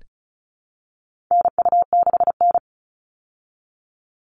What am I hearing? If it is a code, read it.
Morse code, heard as a keyed tone, about 34 wpm, 700 Hz. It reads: NU6N
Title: Call Signs Repeated 3x - 35wpm